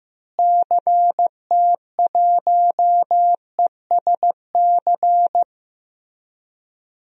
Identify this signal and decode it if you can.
Morse code, heard as a keyed tone, about 15 wpm, 700 Hz. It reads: CT1ESC
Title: DXCC-100 Call Signs - 6 Characters - Repeated 2x - 15wpm